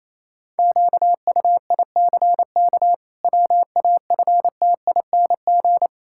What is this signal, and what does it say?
Morse code, heard as a keyed tone, about 28 wpm, 700 Hz. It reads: QUICK WAFTING
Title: Pangram Challenge v2 - 28wpm